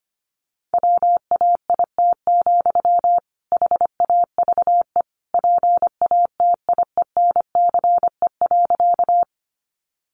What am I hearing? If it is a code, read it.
Morse code, heard as a keyed tone, about 25 wpm, 700 Hz. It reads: WAIT, HAVE PATIENCE.
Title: Sentences from Top 1300 Words 25wpm